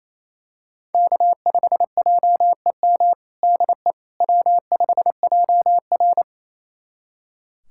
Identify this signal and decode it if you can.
Morse code, heard as a keyed tone, about 28 wpm, 700 Hz. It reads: K5JEM DE W5JR
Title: USA Callsign DE Callsign 28wpm